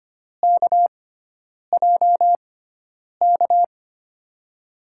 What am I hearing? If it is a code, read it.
Morse code, heard as a keyed tone, about 25 wpm, spaced as for 9 wpm, 700 Hz. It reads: KJK
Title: USA Call Signs - 6 Characters - 6x Character Spacing - 25wpm